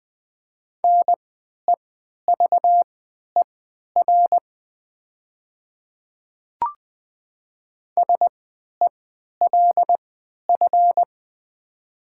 Morse code, heard as a keyed tone, about 20 wpm, spaced as for 11 wpm, 700 Hz. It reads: NEVER SELF
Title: 3x Character Spacing - Top 500 Words - 20wpm